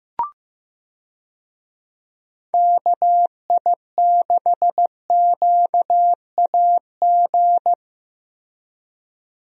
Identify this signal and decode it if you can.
Morse code, heard as a keyed tone, about 15 wpm, 700 Hz. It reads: KI6QAG
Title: USA Call Signs - 6 Characters - Repeated 3x - 15wpm